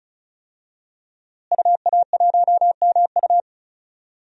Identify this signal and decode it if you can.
Morse code, heard as a keyed tone, about 36 wpm, 700 Hz. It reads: UA1MU
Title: DXCC-100 Call Signs - 5 Characters - Repeated 3x - 35wpm